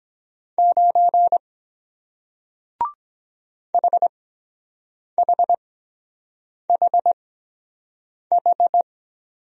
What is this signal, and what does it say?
Morse code, heard as a keyed tone, about 23 wpm, spaced as for 7 wpm, 700 Hz. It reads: 9 HHHH